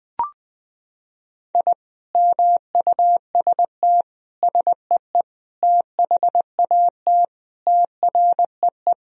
Morse code, heard as a keyed tone, about 20 wpm, 700 Hz. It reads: I MUST SEE THAT TREE
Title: Sentences from Top 1000 Words - Review Part 1 - 20wpm